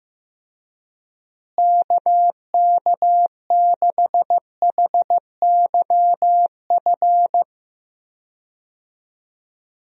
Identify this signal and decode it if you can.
Morse code, heard as a keyed tone, about 15 wpm, 700 Hz. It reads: KK6HYF
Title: USA Call Signs - 6 Characters - Repeated 3x - 15wpm